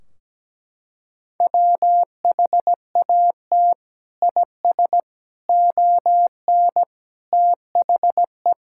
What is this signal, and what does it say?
Morse code, heard as a keyed tone, about 17 wpm, 700 Hz. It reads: WHAT IS ON THE
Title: Sentences from Top 1600 Words 17wpm